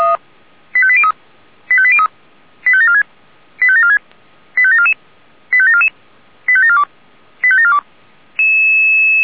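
Five-tone selective call (ZVEI), heard as quick runs of five tones, tones of about 70 ms. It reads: 85803 85803 85746 85746 85744 85744 85742 85742
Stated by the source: ZVEI five-tone selective call received on an SDR.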